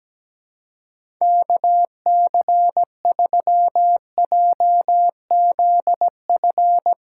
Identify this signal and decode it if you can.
Morse code, heard as a keyed tone, about 17 wpm, 700 Hz. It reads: KC3JZF
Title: USA Callsign DE Callsign 17wpm